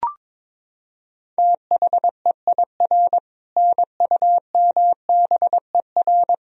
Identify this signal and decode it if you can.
Morse code, heard as a keyed tone, about 21 wpm, 700 Hz. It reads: THEIR NUMBER
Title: Sentences from Top 1000 Words - Review Part 1 - 22wpm